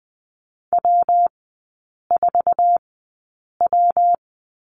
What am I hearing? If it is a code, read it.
Morse code, heard as a keyed tone, about 20 wpm, spaced as for 8 wpm, 700 Hz. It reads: W4W
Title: 114 Lesson 19e - Letter w - Three Characters 20wpm